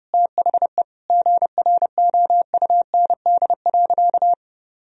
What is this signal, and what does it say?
Morse code, heard as a keyed tone, about 30 wpm, 700 Hz. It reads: THE GROUND.
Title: Sentences from Top 1000 Words - Review Part 3 - 30wpm